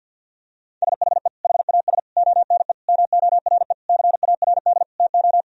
Morse code, heard as a keyed tone, about 50 wpm, 700 Hz. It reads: SHE HAS ONE MORE CARD TO
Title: Sentences from Top 1000 Words - Review Part 4 - 50wpm